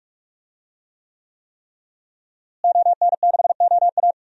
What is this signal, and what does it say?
Morse code, heard as a keyed tone, about 45 wpm, 700 Hz. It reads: ON6OA
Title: DXCC-100 Call Signs - 5 Characters - Repeated 3x - 45wpm